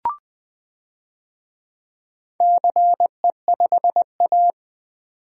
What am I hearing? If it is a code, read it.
Morse code, heard as a keyed tone, about 20 wpm, 700 Hz. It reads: CE5A
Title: DXCC-100 Call Signs - 4 Characters - 20wpm